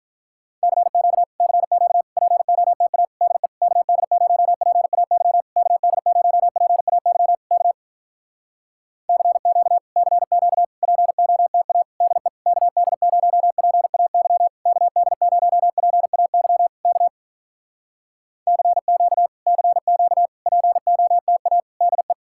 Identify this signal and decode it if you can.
Morse code, heard as a keyed tone, about 46 wpm, 700 Hz. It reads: CQ CQ POTA DE KD0PAY KD0PAY K CQ CQ POTA DE KD0PAY KD0PAY K CQ CQ POTA DE